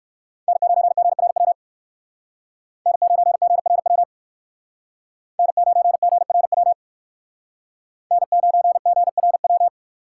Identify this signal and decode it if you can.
Morse code, heard as a keyed tone, about 57 wpm, 700 Hz. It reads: N9GRW N9GRW N9GRW N9GRW